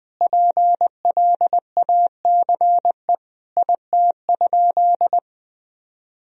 Morse code, heard as a keyed tone, about 21 wpm, 700 Hz. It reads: PLACE IT?
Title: Sentences from Top 200 Words and Abbreviations 20wpm